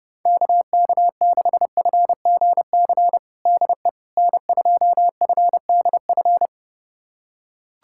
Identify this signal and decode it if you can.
Morse code, heard as a keyed tone, about 32 wpm, 700 Hz. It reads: KK6FGC DE N2FDF
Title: USA Callsign DE Callsign 30wpm